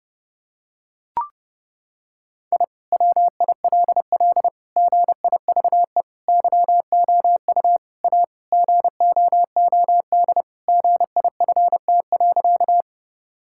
Morse code, heard as a keyed tone, about 30 wpm, 700 Hz. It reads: I WILL GIVE YOU A GOOD GIFT.
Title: Sentences from Top 1000 Words - Review Part 6 - 30wpm